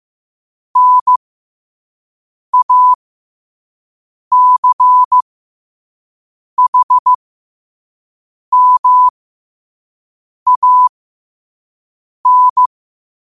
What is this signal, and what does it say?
Morse code, heard as a keyed tone, about 15 wpm, spaced as for 5 wpm, 1000 Hz. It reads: NACHMAN
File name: qotdfiles / 5wpm